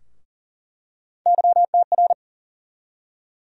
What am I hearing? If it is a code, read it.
Morse code, heard as a keyed tone, about 40 wpm, 700 Hz. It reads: YTR
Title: USA Call Sign Postfixes - 3 Letters 40wpm